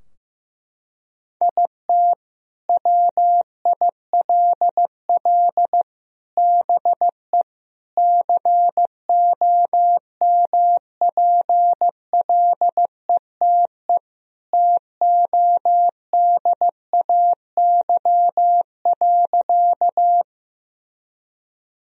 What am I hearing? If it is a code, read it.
Morse code, heard as a keyed tone, about 15 wpm, 700 Hz. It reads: IT WILL BE COMPLETE TODAY.
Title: Sentences from Top 1000 Words - Review Part 2 - 15wpm